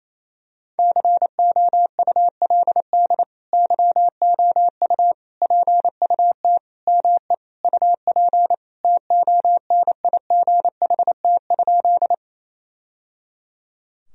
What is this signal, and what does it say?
Morse code, heard as a keyed tone, about 28 wpm, 700 Hz. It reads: COULD YOU PUT ME UP TONIGHT?
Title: Sentences from Top 1000 Words - Review Part 3 - 28wpm